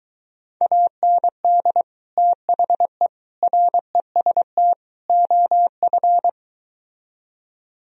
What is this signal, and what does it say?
Morse code, heard as a keyed tone, about 23 wpm, 700 Hz. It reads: AND THE REST OF